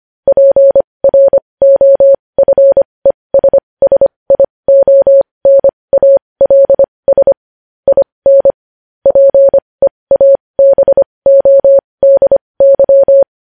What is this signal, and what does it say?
Morse code, heard as a keyed tone, about 25 wpm, spaced as for 20 wpm, 550 Hz. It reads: PROFESSIONALS IN PEABODY
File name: podcast / News headlines-1x-25x20